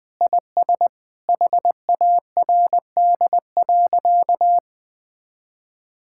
Morse code, heard as a keyed tone, about 20 wpm, 700 Hz. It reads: IS HARD.
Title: Sentences with 4-Word Phrases - 20wpm